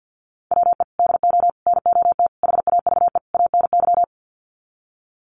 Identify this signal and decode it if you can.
Morse code, heard as a keyed tone, about 50 wpm, 700 Hz. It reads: WE DO NOT HAVE ANY
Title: Sets of 5 Words 50wpm